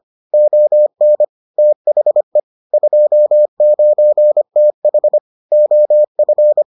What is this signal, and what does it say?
Morse code, heard as a keyed tone, about 25 wpm, 600 Hz. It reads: ON THE 29TH OF